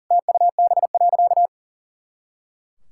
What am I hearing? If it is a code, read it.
Morse code, heard as a keyed tone, about 40 wpm, 700 Hz. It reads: TUB.
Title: Pangram Challenge v2 - 40wpm